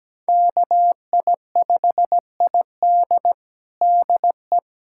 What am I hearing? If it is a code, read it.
Morse code, heard as a keyed tone, about 17 wpm, 700 Hz. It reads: KI5ID DE
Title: USA Callsign DE Callsign - 5 Characters - 17wpm